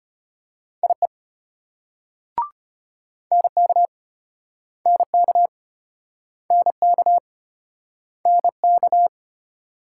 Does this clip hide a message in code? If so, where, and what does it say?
Yes: Morse code from 0.8 to 9.1 s, heard as a keyed tone, about 33 wpm, 700 Hz. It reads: IE NK NK NK NK